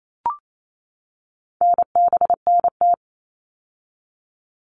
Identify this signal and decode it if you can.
Morse code, heard as a keyed tone, about 28 wpm, 700 Hz. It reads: NBNT
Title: All Cut Numbers - 4-Digit 28wpm